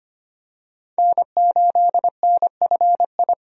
Morse code, heard as a keyed tone, about 25 wpm, 700 Hz. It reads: N8NFI
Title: USA Call Signs - 5 Characters - 25wpm